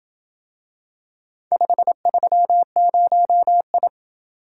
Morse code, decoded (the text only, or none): 530I